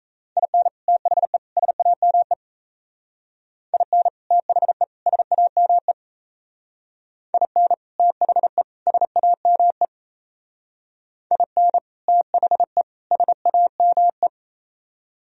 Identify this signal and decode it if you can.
Morse code, heard as a keyed tone, about 36 wpm, 700 Hz. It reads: IN THE SAME IN THE SAME IN THE SAME IN THE SAME